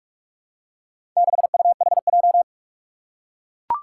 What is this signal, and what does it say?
Morse code, heard as a keyed tone, about 44 wpm, 700 Hz. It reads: 6UHJ
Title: Four Challenging Letter-Number Combinations 45wpm